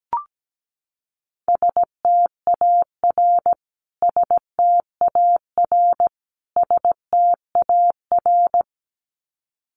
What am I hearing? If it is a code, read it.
Morse code, heard as a keyed tone, about 17 wpm, 700 Hz. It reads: STAR STAR STAR
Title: Top 500 Words Repeated 17wpm